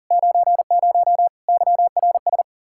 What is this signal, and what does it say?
Morse code, heard as a keyed tone, about 40 wpm, 700 Hz. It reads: 90 YRS
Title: Practice QSO - Exchange 3 - More Info - 40wpm